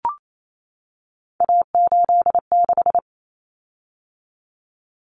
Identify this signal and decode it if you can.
Morse code, heard as a keyed tone, about 28 wpm, 700 Hz. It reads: A86
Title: Common Cut Numbers - 3-Digit 28wpm